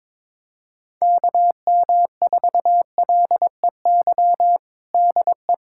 Morse code, heard as a keyed tone, about 22 wpm, 700 Hz. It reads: KM4LEY DE
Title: USA Callsign DE Callsign 22wpm